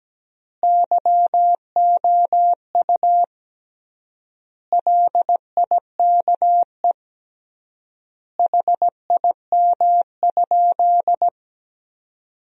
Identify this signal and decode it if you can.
Morse code, heard as a keyed tone, about 17 wpm, 700 Hz. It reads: YOU LIKE HIM?
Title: 3x Word-Spacing - Sentences from Top 100 Words - 17wpm